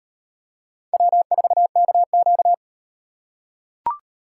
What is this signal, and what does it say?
Morse code, heard as a keyed tone, about 38 wpm, 700 Hz. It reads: W4KQ